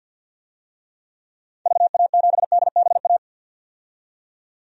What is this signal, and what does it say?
Morse code, heard as a keyed tone, about 50 wpm, 700 Hz. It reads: VA7DBA
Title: DXCC-100 Call Signs - 6 Characters - 50wpm